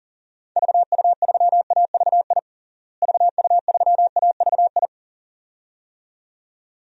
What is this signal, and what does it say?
Morse code, heard as a keyed tone, about 40 wpm, 700 Hz. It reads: VU3AVI VU3AVI
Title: DXCC-100 Call Signs - 6 Characters - Repeated 2x - 40wpm